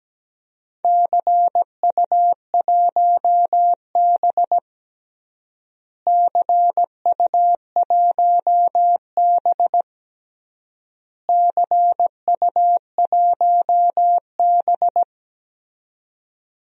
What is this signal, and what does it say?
Morse code, heard as a keyed tone, about 17 wpm, 700 Hz. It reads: CU1B CU1B CU1B
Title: DXCC-100 Call Signs - 4 Characters - Repeated 3x - 17wpm